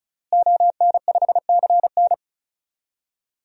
Morse code, heard as a keyed tone, about 36 wpm, 700 Hz. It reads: ON5CN
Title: DXCC-100 Call Signs - 5 Characters - Repeated 3x - 35wpm